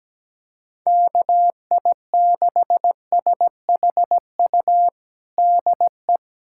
Morse code, heard as a keyed tone, about 17 wpm, 700 Hz. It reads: KI6SHU DE
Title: USA Callsign DE Callsign - 6 Characters - 17wpm